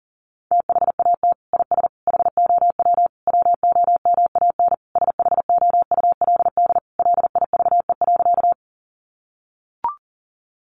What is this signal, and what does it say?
Morse code, heard as a keyed tone, about 40 wpm, 700 Hz. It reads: THAT IS HOW WOMAN SHOULD LIVE.
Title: Sentences from Top 1100 Words 40wpm